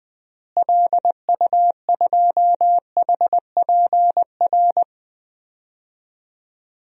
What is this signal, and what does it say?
Morse code, heard as a keyed tone, about 20 wpm, 700 Hz. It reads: LU2HPR
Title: DXCC-100 Call Signs - 6 Characters - Repeated 2x - 20wpm